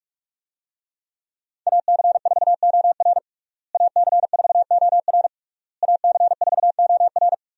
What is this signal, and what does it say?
Morse code, heard as a keyed tone, about 46 wpm, 700 Hz. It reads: AC4OR AC4OR AC4OR
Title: USA Call Signs - 5 Characters - Repeated 3x - 45wpm